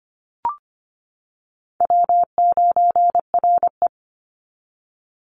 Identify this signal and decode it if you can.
Morse code, heard as a keyed tone, about 25 wpm, 700 Hz. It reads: W9RE
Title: Easy Call Signs - 25wpm